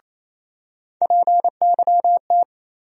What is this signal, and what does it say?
Morse code, heard as a keyed tone, about 27 wpm, 700 Hz. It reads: PYT
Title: USA Call Sign Postfixes - 3 Letters 28wpm